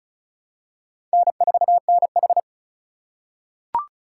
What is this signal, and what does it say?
Morse code, heard as a keyed tone, about 34 wpm, 700 Hz. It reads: N4NH
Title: USA Call Signs - 4 Characters - 3x Character Spacing - 35wpm